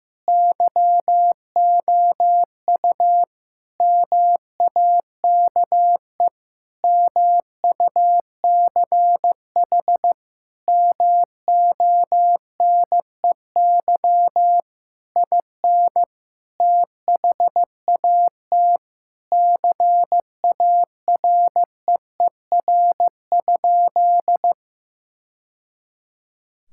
Morse code, heard as a keyed tone, about 15 wpm, 700 Hz. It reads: YOU MAKE MUCH MONEY IN THAT CAREER?
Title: Sentences from Top 1000 Words - Review Part 6 - 15wpm